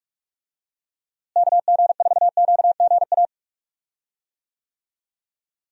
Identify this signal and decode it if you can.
Morse code, heard as a keyed tone, about 47 wpm, 700 Hz. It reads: KG4QGA
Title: USA Call Signs - 6 Characters - Repeated 3x - 45wpm